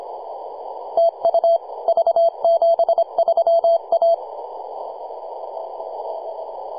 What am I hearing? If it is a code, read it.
Morse code, heard as a keyed tone, about 24 wpm, 670 Hz. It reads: TU V73A
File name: posts / 2015-sept-26-2040z-v73a-21-025-cw